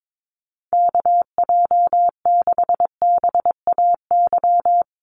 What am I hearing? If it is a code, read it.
Morse code, heard as a keyed tone, about 22 wpm, 700 Hz. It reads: KJ6BAY
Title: USA Call Signs 22wpm